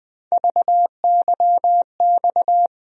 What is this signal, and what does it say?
Morse code, heard as a keyed tone, about 20 wpm, 700 Hz. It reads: VYX
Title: USA Call Sign Postfixes - 3 Letters 20wpm